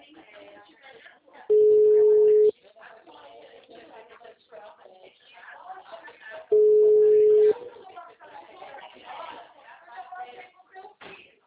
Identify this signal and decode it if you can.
Ringback tone (Europe, 425 Hz), with speech.
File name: out- -2032- - - .55902